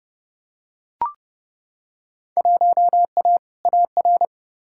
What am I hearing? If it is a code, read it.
Morse code, heard as a keyed tone, about 30 wpm, 700 Hz. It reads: 1A AR
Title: Field Day Exchanges Encore - 30wpm